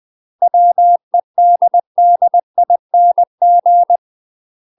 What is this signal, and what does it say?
Morse code, heard as a keyed tone, about 20 wpm, 700 Hz. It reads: WEDDING
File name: UK MASTER QSOs TRAINING / 20WPM / UK MASTER QSOs 0058